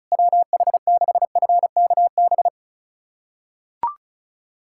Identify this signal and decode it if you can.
Morse code, heard as a keyed tone, about 36 wpm, 700 Hz. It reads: WH6FKB
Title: USA Callsign DE Callsign - 6 Characters - 35wpm